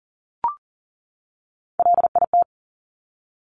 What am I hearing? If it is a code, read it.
Morse code, heard as a keyed tone, about 42 wpm, 700 Hz. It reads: LIT
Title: Top Three Letter Combinations in Words 40wpm